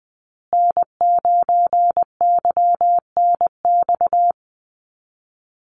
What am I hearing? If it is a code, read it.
Morse code, heard as a keyed tone, about 20 wpm, 700 Hz. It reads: N9YNX
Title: USA Call Signs Repeated 2x 20wpm